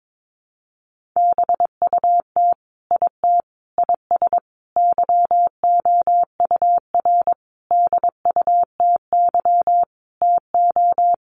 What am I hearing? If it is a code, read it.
Morse code, heard as a keyed tone, about 22 wpm, 700 Hz. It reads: BUT IT IS YOUR DUTY TO
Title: Sentences from Top 1500 Words 22wpm